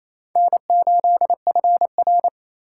Morse code, heard as a keyed tone, about 27 wpm, 700 Hz. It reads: N8FR
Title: USA Call Signs - 4 Characters - 4x Character Spacing - 28wpm